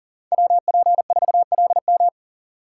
Morse code, heard as a keyed tone, about 40 wpm, 700 Hz. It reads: WP4LM
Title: USA Call Signs - 4x Character Spacing - 40wpm